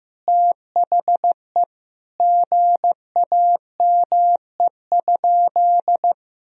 Morse code, heard as a keyed tone, about 15 wpm, 700 Hz. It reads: THE GAME?
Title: Sentences from Top 1000 Words - Review Part 4 - 15wpm